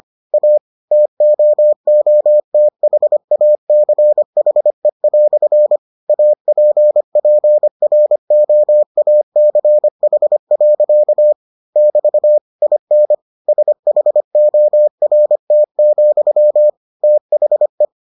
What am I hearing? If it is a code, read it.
Morse code, heard as a keyed tone, about 25 wpm, 600 Hz. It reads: A TOOTHACHE" APPROACH. <BT> IN SHORT, THE